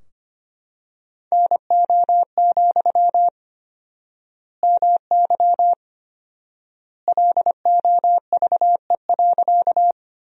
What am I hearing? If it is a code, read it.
Morse code, heard as a keyed tone, about 25 wpm, 700 Hz. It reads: NO, MY LOVE.
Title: 4x Word-Spacing - Sentences from Top 400 Words - 25wpm